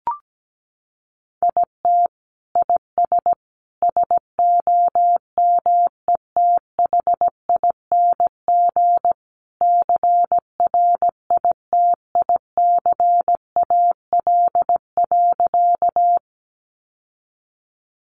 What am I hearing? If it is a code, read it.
Morse code, heard as a keyed tone, about 17 wpm, 700 Hz. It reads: IT IS SOMETHING CRITICAL.
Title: Sentences from Top 1100 Words 17wpm